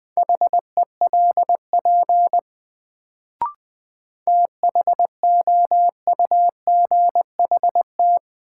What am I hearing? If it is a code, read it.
Morse code, heard as a keyed tone, about 20 wpm, 700 Hz. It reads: HELP THOUGHT
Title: Sets of 2 Words - New Word Drill - 20wpm